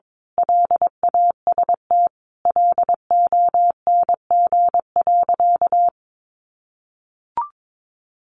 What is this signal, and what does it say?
Morse code, heard as a keyed tone, about 22 wpm, 700 Hz. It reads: LAST LONG.
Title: Sentences from Top 1300 Words 22wpm